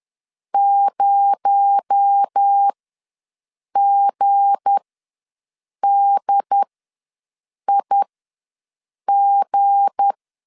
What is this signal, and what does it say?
Morse code, heard as a keyed tone, about 10 wpm, spaced as for 6 wpm, 800 Hz. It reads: 0GDIG
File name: LEZIONI / 7 GF9Z / 7 G ws 10wpm 3843